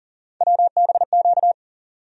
Morse code, heard as a keyed tone, about 40 wpm, 700 Hz. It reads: WBQ
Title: USA Call Sign Postfixes - 3 Letters 40wpm